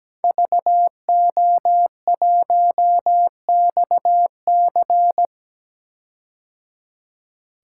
Morse code, heard as a keyed tone, about 17 wpm, 700 Hz. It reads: VO1XC
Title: K1USN SST QSO Practice - 17wpm